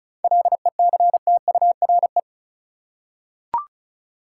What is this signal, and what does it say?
Morse code, heard as a keyed tone, about 34 wpm, 700 Hz. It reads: LECTURE
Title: Seven Letter Words - 35wpm